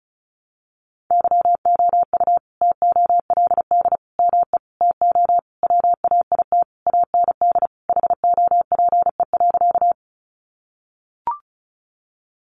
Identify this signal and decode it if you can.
Morse code, heard as a keyed tone, about 34 wpm, 700 Hz. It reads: YOU TOLD ME TO WAIT AND HOPE.
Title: Sentences from Top 1000 Words 35wpm